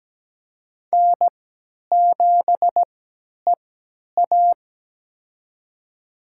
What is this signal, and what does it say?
Morse code, heard as a keyed tone, about 17 wpm, spaced as for 10 wpm, 700 Hz. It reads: N7EA
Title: USA Call Signs - 4 Characters - 3x Character Spacing - 17wpm